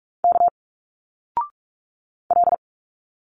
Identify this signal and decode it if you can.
Morse code, heard as a keyed tone, about 46 wpm, 700 Hz. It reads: K L
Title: Single Letters 45wpm